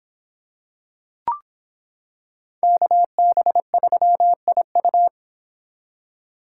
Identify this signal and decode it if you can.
Morse code, heard as a keyed tone, about 25 wpm, 700 Hz. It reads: KB3IU